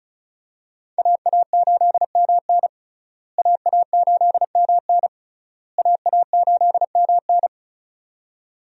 Morse code, heard as a keyed tone, about 34 wpm, 700 Hz. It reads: AA8MN AA8MN AA8MN
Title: USA Call Signs - 5 Characters - Repeated 3x - 35wpm